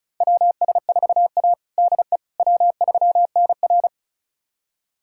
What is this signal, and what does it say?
Morse code, heard as a keyed tone, about 36 wpm, 700 Hz. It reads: WS4A DE W3NR
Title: USA Callsign DE Callsign - 4 Characters - 35wpm